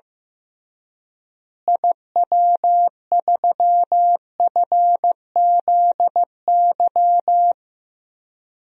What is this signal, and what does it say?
Morse code, heard as a keyed tone, about 15 wpm, 700 Hz. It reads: IW3FZY
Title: DXCC-100 Call Signs - 6 Characters - Repeated 3x - 15wpm